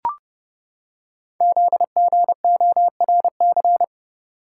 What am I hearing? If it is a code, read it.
Morse code, heard as a keyed tone, about 30 wpm, 700 Hz. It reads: ZGORC
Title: Code Groups - 5 Letters - 30wpm